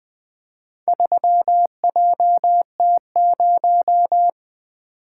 Morse code, heard as a keyed tone, about 20 wpm, 700 Hz. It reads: 3JT0
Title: Four Letter-Number Combinations - 20wpm